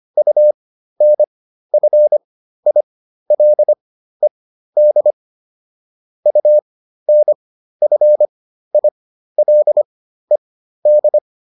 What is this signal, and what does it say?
Morse code, heard as a keyed tone, about 25 wpm, spaced as for 13 wpm, 600 Hz. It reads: UNFILED UNFILED